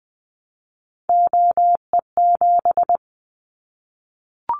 Morse code, heard as a keyed tone, about 20 wpm, 700 Hz. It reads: OE7
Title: Callsign Prefixes for Top 50 DXCC - 20wpm